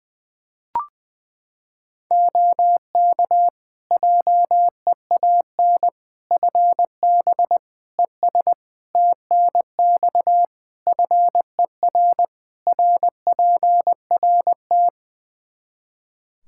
Morse code, heard as a keyed tone, about 20 wpm, 700 Hz. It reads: OK JEAN FB ES TNX FER RPRT
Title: Practice QSO - Exchange 2 - Info - 20wpm